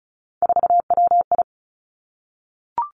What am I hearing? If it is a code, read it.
Morse code, heard as a keyed tone, about 36 wpm, 700 Hz. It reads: 4WI
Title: Three Challenging Letter-Number Combinations 35wpm